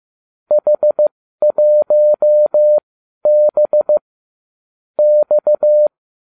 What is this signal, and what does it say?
Morse code, heard as a keyed tone, about 15 wpm, spaced as for 12 wpm, 600 Hz. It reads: H1B X